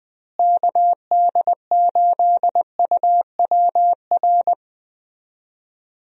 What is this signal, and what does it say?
Morse code, heard as a keyed tone, about 20 wpm, 700 Hz. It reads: KD8UWR
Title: USA Call Signs - 6 Characters - Repeated 2x - 20wpm